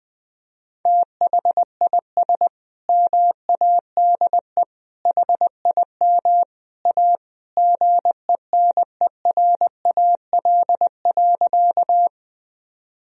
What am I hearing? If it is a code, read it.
Morse code, heard as a keyed tone, about 21 wpm, 700 Hz. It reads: THIS MADE HIM A GENERAL.
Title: Sentences from Top 1000 Words - Review Part 3 - 20wpm